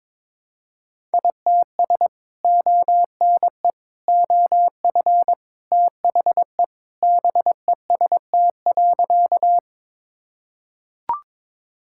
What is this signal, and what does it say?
Morse code, heard as a keyed tone, about 22 wpm, 700 Hz. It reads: ITS ONE OF THE BEST.
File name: practice / 22wpm / Sentences with 3-Word Phrases - 22wpm